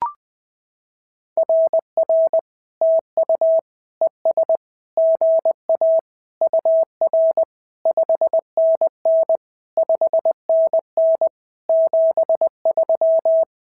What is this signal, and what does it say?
Morse code, heard as a keyed tone, about 20 wpm, 650 Hz. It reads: RR TU ES GA UR 5NN 5NN 73
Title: SOTA QSO Practice 20wpm